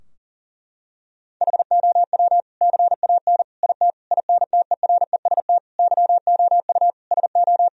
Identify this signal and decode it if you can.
Morse code, heard as a keyed tone, about 40 wpm, 700 Hz. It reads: HOW CAN IT INTEREST YOU SO
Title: Sentences from Top 1000 Words - Review Part 2 - 40wpm